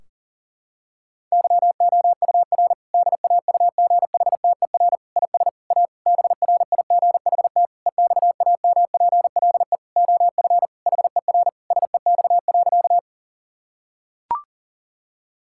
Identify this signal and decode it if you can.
Morse code, heard as a keyed tone, about 40 wpm, 700 Hz. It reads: YOUR DAUGHTER IS A BRIGHT EXAMPLE OF HER SEX.
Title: Sentences from Top 1600 Words 40wpm